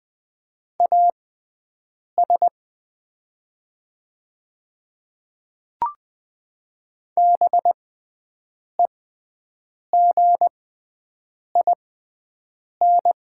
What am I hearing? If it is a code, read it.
Morse code, heard as a keyed tone, about 20 wpm, spaced as for 7 wpm, 700 Hz. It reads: AS BEGIN
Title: 6x Character Spacing - Top 500 Words - 20wpm